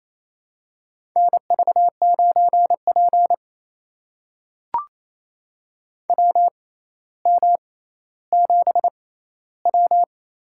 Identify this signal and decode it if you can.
Morse code, heard as a keyed tone, about 28 wpm, spaced as for 14 wpm, 700 Hz. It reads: NV9P WM7W
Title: USA Call Signs - 4 Characters - 6x Character Spacing - 28wpm